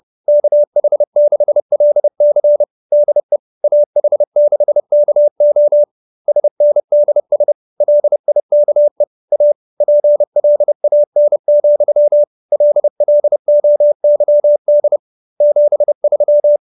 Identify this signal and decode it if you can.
Morse code, heard as a keyed tone, about 30 wpm, 590 Hz. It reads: KH6LC DE AH6KO SNDS LIKE A PLAN, LLOYD 73